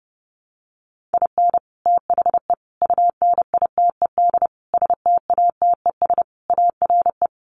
Morse code, heard as a keyed tone, about 29 wpm, 700 Hz. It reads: IN THE UNITED STATES ARE
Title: Sets of 5 Words Encore 30wpm